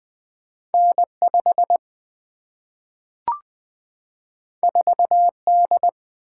Morse code, heard as a keyed tone, about 20 wpm, 700 Hz. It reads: N5 4D
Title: Single-Letter Single-Number 20wpm